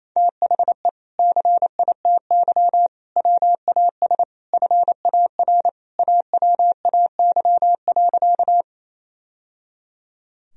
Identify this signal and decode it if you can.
Morse code, heard as a keyed tone, about 28 wpm, 700 Hz. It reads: THE CITY WAS FAR AWAY.
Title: Sentences from Top 1000 Words - Review Part 1 - 28wpm